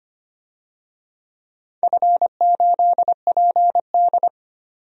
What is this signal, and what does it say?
Morse code, heard as a keyed tone, about 25 wpm, 700 Hz. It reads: F8PD
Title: DXCC-100 Call Signs - 4 Characters - 25wpm